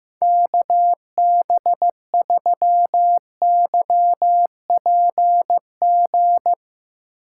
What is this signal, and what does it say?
Morse code, heard as a keyed tone, about 16 wpm, 700 Hz. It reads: KB3YPG
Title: Speed Racing - USA Call Signs 15wpm